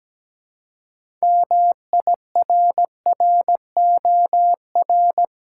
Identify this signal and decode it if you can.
Morse code, heard as a keyed tone, about 17 wpm, 700 Hz. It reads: MIRROR
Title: Six Letter Words - 17wpm